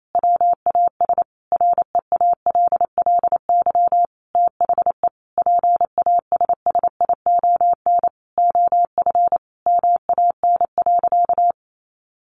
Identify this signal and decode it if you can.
Morse code, heard as a keyed tone, about 28 wpm, 700 Hz. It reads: WAS REALLY THE PASSION OF MAN.
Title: Sentences from Top 1400 Words 28wpm